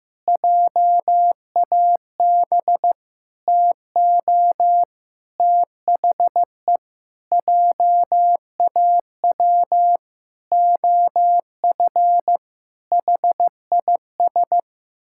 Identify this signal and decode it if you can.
Morse code, heard as a keyed tone, about 15 wpm, 700 Hz. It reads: JAB TO THE JAW OF HIS
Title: Pangram Challenge v2 - 15wpm